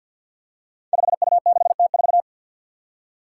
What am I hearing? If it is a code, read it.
Morse code, heard as a keyed tone, about 49 wpm, 700 Hz. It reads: 5U6T4
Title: Five Challenging Letter-Number Combinations 50wpm